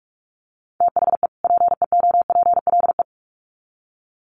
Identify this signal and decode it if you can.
Morse code, heard as a keyed tone, about 46 wpm, 700 Hz. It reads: THE PEOPLE
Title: Sets of 2 Words 45wpm